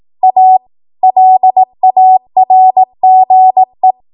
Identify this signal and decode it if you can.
Morse code, heard as a keyed tone, about 18 wpm, 750 Hz. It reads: A LARGE
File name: Morse / 18WPM